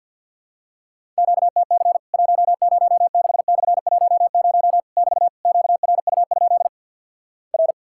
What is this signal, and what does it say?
Morse code, heard as a keyed tone, about 52 wpm, 700 Hz. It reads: QTC 106/10 <BT> QRU? R
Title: Worked All Europe DX Contest - QTC Exchange with Repeats - 50wpm